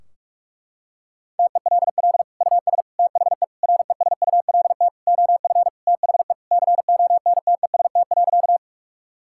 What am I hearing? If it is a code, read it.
Morse code, heard as a keyed tone, about 46 wpm, 700 Hz. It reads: TELL US THE RESULT OF THE CONTEST.
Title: Sentences from Top 1600 Words 45wpm